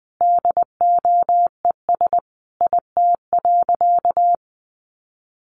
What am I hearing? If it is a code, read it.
Morse code, heard as a keyed tone, about 20 wpm, 700 Hz. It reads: DOES IT.
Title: Sentences from Top 1500 Words 20wpm